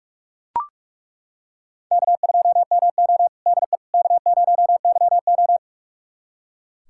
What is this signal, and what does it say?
Morse code, heard as a keyed tone, about 42 wpm, 700 Hz. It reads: K2MO DE K0YO
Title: USA Callsign DE Callsign - 4 Characters - 45wpm